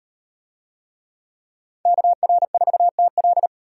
Morse code, heard as a keyed tone, about 34 wpm, 700 Hz. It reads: KR4TL